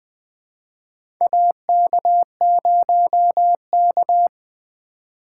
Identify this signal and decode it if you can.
Morse code, heard as a keyed tone, about 20 wpm, 700 Hz. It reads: AK0K
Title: USA Call Signs - 4 Characters - 4x Character Spacing - 20wpm